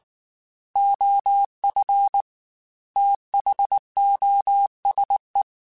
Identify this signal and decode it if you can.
Morse code, heard as a keyed tone, about 19 wpm, 800 Hz. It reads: OF THOSE
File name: Technical Subjects / Technical Subjects-mp3s / OR Gate @18